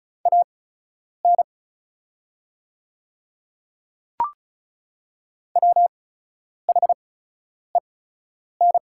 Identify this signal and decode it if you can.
Morse code, heard as a keyed tone, about 36 wpm, spaced as for 10 wpm, 700 Hz. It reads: AN WHEN
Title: 8x Character Spacing - Top 300 Words - 35wpm